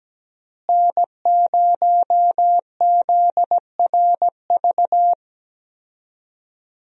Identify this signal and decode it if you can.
Morse code, heard as a keyed tone, about 17 wpm, 700 Hz. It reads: N0ZRV
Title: USA Call Signs - 5 Characters - Repeated 3x - 17wpm